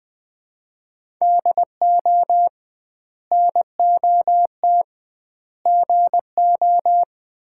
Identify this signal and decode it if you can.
Morse code, heard as a keyed tone, about 20 wpm, 700 Hz. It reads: DO NOT GO
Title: 2x Word-Spacing - Sets of 4 Words - 20wpm